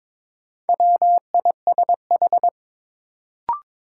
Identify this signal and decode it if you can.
Morse code, heard as a keyed tone, about 21 wpm, 700 Hz. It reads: WISH
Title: Sentences from Top 1600 Words - New Word Drill Alt - 22wpm